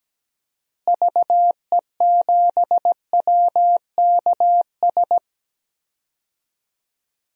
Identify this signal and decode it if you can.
Morse code, heard as a keyed tone, about 17 wpm, 700 Hz. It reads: VE7WKS
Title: DXCC-100 Call Signs - 6 Characters - Repeated 3x - 17wpm